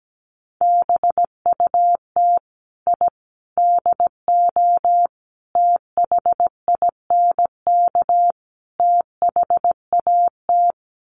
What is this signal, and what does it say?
Morse code, heard as a keyed tone, about 17 wpm, 700 Hz. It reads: BUT I DO THINK THAT
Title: Sets of 5 Words Encore2 17wpm